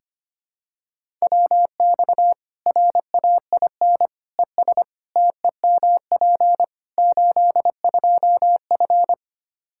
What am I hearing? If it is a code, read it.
Morse code, heard as a keyed tone, about 25 wpm, 700 Hz. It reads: WX RAIN ES TEMP 82F
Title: Practice QSO - Exchange 2 - Info - 25wpm